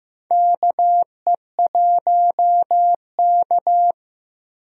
Morse code, heard as a keyed tone, about 15 wpm, 700 Hz. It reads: KE1K
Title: USA Call Signs - 4 Characters - 15wpm